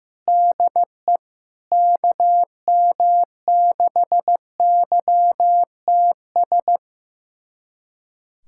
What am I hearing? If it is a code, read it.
Morse code, heard as a keyed tone, about 16 wpm, 700 Hz. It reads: DE KM6YTS
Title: USA Callsign DE Callsign - 6 Characters - 15wpm